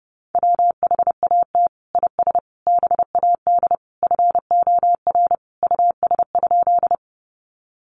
Morse code, heard as a keyed tone, about 30 wpm, 700 Hz. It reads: WHAT IS BAD FOR US?
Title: Sentences from Top 700 Words 30wpm